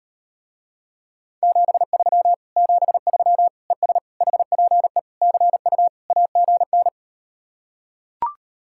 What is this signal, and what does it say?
Morse code, heard as a keyed tone, about 38 wpm, 700 Hz. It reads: 73 73 ES HPE CU AGN